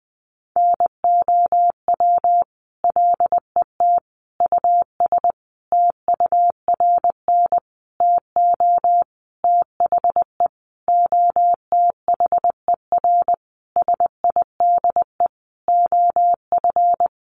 Morse code, heard as a keyed tone, about 20 wpm, 700 Hz. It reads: NOW LET US TURN TO THE OTHER SIDE OF
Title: Sentences from Top 1400 Words 20wpm